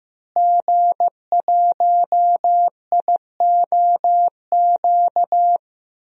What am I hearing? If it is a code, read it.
Morse code, heard as a keyed tone, about 15 wpm, 700 Hz. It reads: G1IOQ
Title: DXCC-100 Call Signs - 5 Characters - Repeated 3x - 15wpm